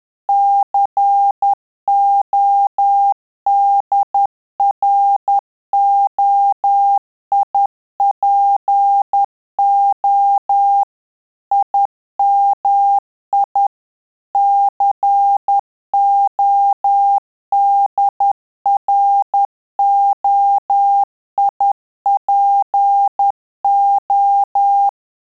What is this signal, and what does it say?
Morse code, heard as a keyed tone, about 10 wpm, 800 Hz. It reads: CODROIPO IMI CODROIPO
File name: Telegraphy-group / QSO CW 50 CPM